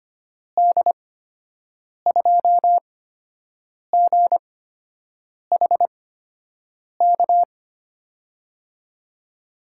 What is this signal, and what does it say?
Morse code, heard as a keyed tone, about 25 wpm, spaced as for 7 wpm, 700 Hz. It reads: D2GHK
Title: USA Call Signs - 6 Characters - 8x Character Spacing - 25wpm